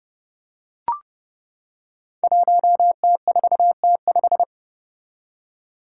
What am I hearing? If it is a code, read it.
Morse code, heard as a keyed tone, about 30 wpm, 700 Hz. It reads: 1T4T5
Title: Five Challenging Letter-Number Combinations 30wpm